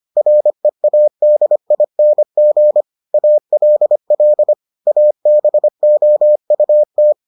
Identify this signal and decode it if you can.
Morse code, heard as a keyed tone, about 25 wpm, 600 Hz. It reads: READING ALL ABOUT